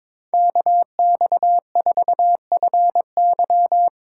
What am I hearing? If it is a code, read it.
Morse code, heard as a keyed tone, about 22 wpm, 700 Hz. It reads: KX4FY
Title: USA Call Signs - 5 Characters - 6x Character Spacing - 22wpm